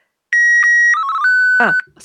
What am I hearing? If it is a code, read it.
SSTV (slow-scan television), Martin 1.